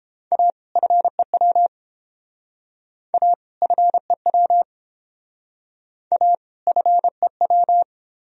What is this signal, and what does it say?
Morse code, heard as a keyed tone, about 29 wpm, 700 Hz. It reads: A FEW A FEW A FEW